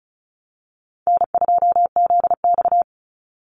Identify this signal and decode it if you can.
Morse code, heard as a keyed tone, about 34 wpm, 700 Hz. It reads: N2ZX
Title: Call Signs - 35wpm